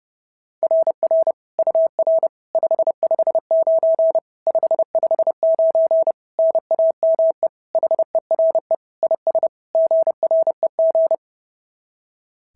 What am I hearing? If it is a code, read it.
Morse code, heard as a keyed tone, about 28 wpm, 660 Hz. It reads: RR UR 559 559 NAME HERE IS GREG
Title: SOTA QSO Practice 30wpm